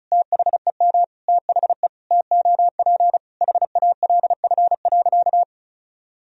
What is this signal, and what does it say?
Morse code, heard as a keyed tone, about 36 wpm, 700 Hz. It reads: THEM THE TOP HALF.
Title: Sentences from Top 1000 Words - Review Part 2 - 35wpm